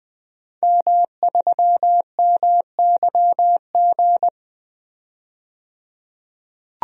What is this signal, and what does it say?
Morse code, heard as a keyed tone, about 20 wpm, 700 Hz. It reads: M3MYG
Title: DXCC-100 Call Signs - 5 Characters - 20wpm